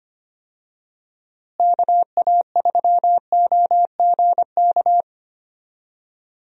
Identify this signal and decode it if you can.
Morse code, heard as a keyed tone, about 25 wpm, 700 Hz. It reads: KA3OGK
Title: USA Call Signs - 6 Characters - Repeated 3x - 25wpm